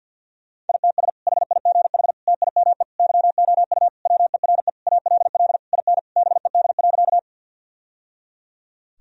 Morse code, heard as a keyed tone, about 51 wpm, 700 Hz. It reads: ITS HIGH TIME YOU WERE ALL IN BED.
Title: Sentences from Top 1000 Words - Review Part 1 - 50wpm